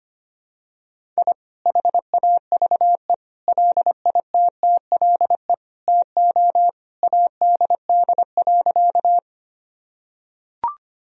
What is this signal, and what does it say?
Morse code, heard as a keyed tone, about 25 wpm, 700 Hz. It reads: I HAVE LITTLE TO ADD.
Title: Sentences from Top 1000 Words - Review Part 1 - 25wpm